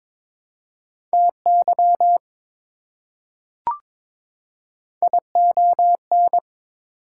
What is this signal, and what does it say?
Morse code, heard as a keyed tone, about 22 wpm, 700 Hz. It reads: TY ION
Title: Prefixes and Suffixes 22wpm